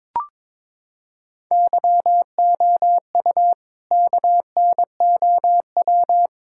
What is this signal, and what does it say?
Morse code, heard as a keyed tone, about 22 wpm, 700 Hz. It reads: YOU KNOW
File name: practice / 22wpm / Sets of 8 Words - 22wpm